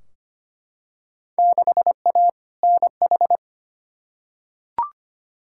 Morse code, heard as a keyed tone, about 25 wpm, 700 Hz. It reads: 6A NH
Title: Field Day Exchanges - 25wpm